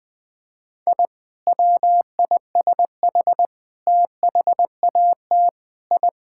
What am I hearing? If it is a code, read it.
Morse code, heard as a keyed tone, about 21 wpm, 700 Hz. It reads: I WISH THAT I
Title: Sentences with 2-Word Phrases - 20wpm